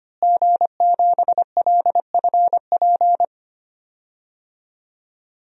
Morse code, heard as a keyed tone, about 25 wpm, 700 Hz. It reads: G7LFP
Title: DXCC-100 Call Signs - 5 Characters - Repeated 3x - 25wpm